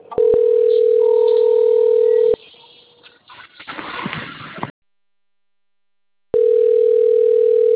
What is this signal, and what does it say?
Ringback tone (North America), 440 + 480 Hz.